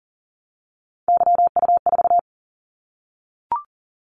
Morse code, heard as a keyed tone, about 40 wpm, 700 Hz. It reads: YU4
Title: Callsign Prefixes for Top 25 DXCC - 40wpm